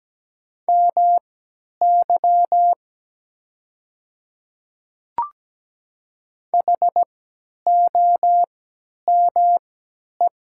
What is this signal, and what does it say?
Morse code, heard as a keyed tone, about 17 wpm, spaced as for 10 wpm, 700 Hz. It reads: MY HOME
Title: 3x Character Spacing - Top 300 Words - 17wpm